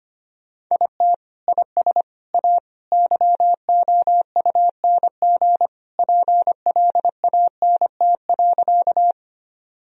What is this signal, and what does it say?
Morse code, heard as a keyed tone, about 25 wpm, 700 Hz. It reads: IT IS A YOUNG PLANT.
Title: Sentences from Top 1000 Words - Review Part 1 - 25wpm